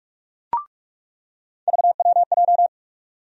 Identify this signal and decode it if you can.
Morse code, heard as a keyed tone, about 46 wpm, 700 Hz. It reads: VWJ
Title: USA Call Sign Postfixes - 3 Letters 45wpm